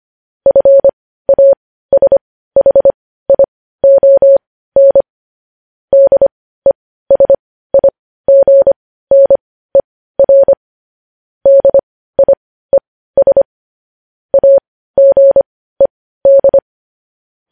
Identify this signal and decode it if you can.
Morse code, heard as a keyed tone, about 25 wpm, spaced as for 15 wpm, 550 Hz. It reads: FASHION DESIGNER DIES AGED
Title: News Headlines at 15